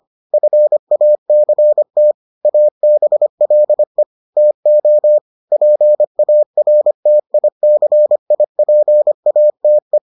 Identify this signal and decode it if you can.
Morse code, heard as a keyed tone, about 25 wpm, 600 Hz. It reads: FACT ABLE TO PARTICIPATE